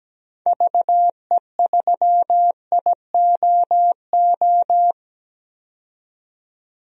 Morse code, heard as a keyed tone, about 17 wpm, 700 Hz. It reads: VE3IOO
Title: DXCC-100 Call Signs - 6 Characters - 17wpm